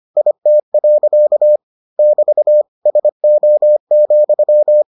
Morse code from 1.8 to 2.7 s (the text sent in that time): <BT>